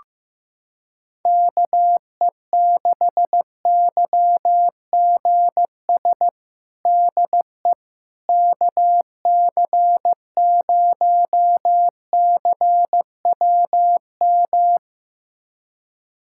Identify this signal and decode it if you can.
Morse code, heard as a keyed tone, about 16 wpm, 700 Hz. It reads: KE6YGS DE KC0CWM
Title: USA Callsign DE Callsign 15wpm